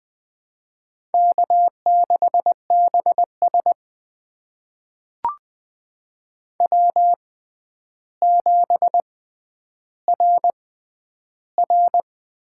Morse code, heard as a keyed tone, about 20 wpm, spaced as for 10 wpm, 700 Hz. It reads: K6BS W7RR
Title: USA Call Signs - 4 Characters - 6x Character Spacing - 20wpm